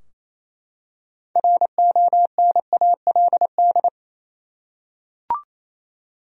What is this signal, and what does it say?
Morse code, heard as a keyed tone, about 28 wpm, 700 Hz. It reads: RONALD
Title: USA Common Names - 28wpm